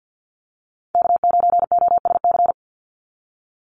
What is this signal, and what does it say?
Morse code, heard as a keyed tone, about 48 wpm, 700 Hz. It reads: K9OSC
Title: NA SOTA Chaser Callsigns 50wpm